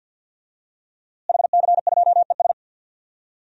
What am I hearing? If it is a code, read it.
Morse code, heard as a keyed tone, about 47 wpm, 700 Hz. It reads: HC2ES
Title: DXCC-100 Call Signs - 5 Characters - Repeated 3x - 50wpm